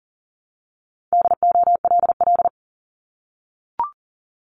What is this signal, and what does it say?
Morse code, heard as a keyed tone, about 38 wpm, 700 Hz. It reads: DOLL
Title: Multi-letter Phonemes Repeated 3x 40wpm